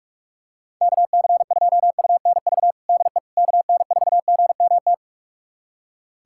Morse code, heard as a keyed tone, about 46 wpm, 700 Hz. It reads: KC2UNV DE KN4GMT
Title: USA Callsign DE Callsign - 6 Characters - 45wpm